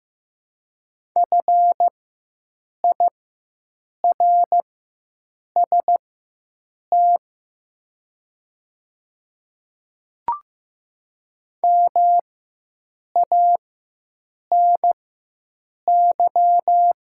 Morse code, heard as a keyed tone, about 15 wpm, spaced as for 7 wpm, 700 Hz. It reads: FIRST MANY
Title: 4x Character Spacing - Top 200 Words - 15wpm